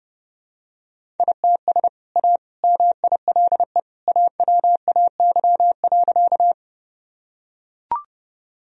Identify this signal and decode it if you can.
Morse code, heard as a keyed tone, about 32 wpm, 700 Hz. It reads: ITS A MILE AWAY.
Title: Sentences from Top 1000 Words - Review Part 1 - 30wpm